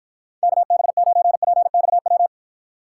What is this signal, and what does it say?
Morse code, heard as a keyed tone, about 53 wpm, 700 Hz. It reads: KB9PXW